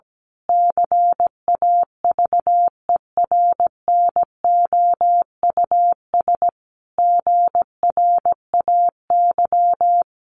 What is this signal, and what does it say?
Morse code, heard as a keyed tone, about 17 wpm, 700 Hz. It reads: CAVERNOUS GRAY
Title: A Sound Of Thunder 17wpm